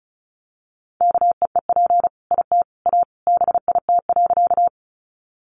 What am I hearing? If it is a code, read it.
Morse code, heard as a keyed tone, about 34 wpm, 700 Hz. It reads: KEEP IT A BIT.
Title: Sentences from Top 900 Words 35wpm